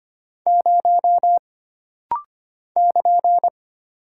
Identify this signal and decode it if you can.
Morse code, heard as a keyed tone, about 25 wpm, 700 Hz. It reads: 0 <KN>
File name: Single Character (Letter:Number:Punctuation:Prosign) - Rapid-Fire - 25wpm